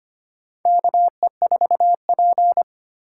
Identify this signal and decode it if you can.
Morse code, heard as a keyed tone, about 25 wpm, 700 Hz. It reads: KE4P
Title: USA Call Signs - 4x Character Spacing - 25wpm